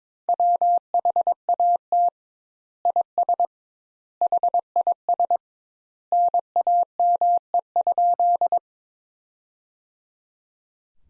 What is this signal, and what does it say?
Morse code, heard as a keyed tone, about 22 wpm, 700 Hz. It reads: WHAT IS HIS NAME?
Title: Sentences Top 200 Words - 22wpm